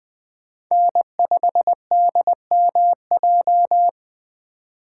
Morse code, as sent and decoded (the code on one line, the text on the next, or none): -. ..... -.. -- .---
N5DMJ